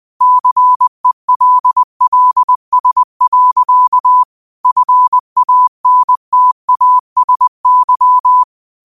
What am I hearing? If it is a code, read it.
Morse code, heard as a keyed tone, about 20 wpm, 1000 Hz. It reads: CELLS. FANTASY
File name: qotdfiles / 20wpm